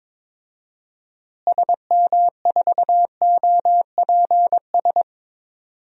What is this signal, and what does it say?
Morse code, heard as a keyed tone, about 22 wpm, 700 Hz. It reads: SM4OPS
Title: DXCC-100 Call Signs - 6 Characters - Repeated 3x - 22wpm